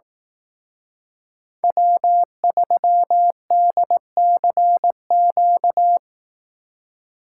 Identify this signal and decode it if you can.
Morse code, heard as a keyed tone, about 18 wpm, 700 Hz. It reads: W3DCQ
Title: Speed Racing - USA 1x3 Call Signs 15wpm